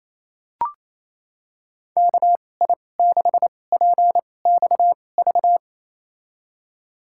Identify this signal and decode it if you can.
Morse code, heard as a keyed tone, about 28 wpm, spaced as for 20 wpm, 700 Hz. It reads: KI6PXV
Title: USA Call Signs - 6 Characters - 2x Character Spacing - 28wpm